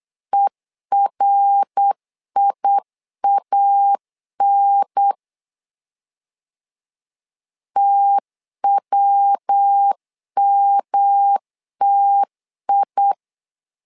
Morse code, heard as a keyed tone, about 8 wpm, 800 Hz. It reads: ERIAN TWMTI
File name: LEZIONI / 3 MWRO / 3 MWR 8wpm 2444